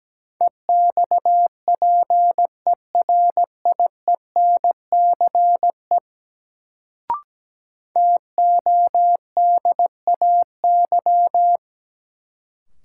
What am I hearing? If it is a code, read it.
Morse code, heard as a keyed tone, about 17 wpm, 700 Hz. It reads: EXPERIENCE TODAY
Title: Sentences from Top 600 Words - New Word Drill - 17wpm